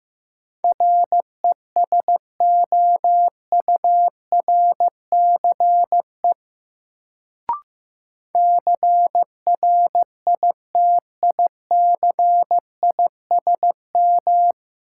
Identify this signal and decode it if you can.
Morse code, heard as a keyed tone, about 15 wpm, 700 Hz. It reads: RESOURCE CRITICISM
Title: Sentences from Top 1300 Words - New Word Drill - 15wpm